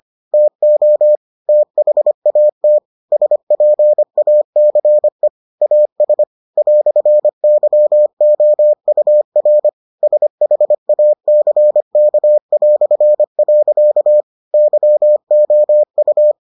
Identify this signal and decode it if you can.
Morse code, heard as a keyed tone, about 25 wpm, 600 Hz. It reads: TO THAT SPACE AS "YOUR SHACK". YOU